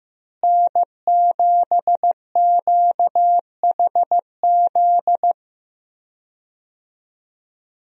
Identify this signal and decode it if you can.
Morse code, heard as a keyed tone, about 15 wpm, 700 Hz. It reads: N7QHZ
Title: USA Call Signs - 5 Characters - Repeated 2x - 15wpm